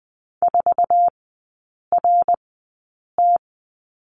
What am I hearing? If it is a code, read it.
Morse code, heard as a keyed tone, about 20 wpm, spaced as for 8 wpm, 700 Hz. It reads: 4RT
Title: 060 Lesson 11e - Letter h - Three Characters 20wpm